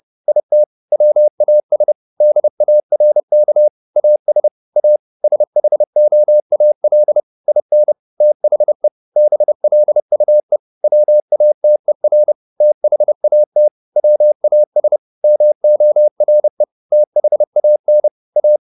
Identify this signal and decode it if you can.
Morse code, heard as a keyed tone, about 30 wpm, 600 Hz. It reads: IT WAS DARK AS A SHOAL IN THE BLUE WATER THAT WAS MORE THAN A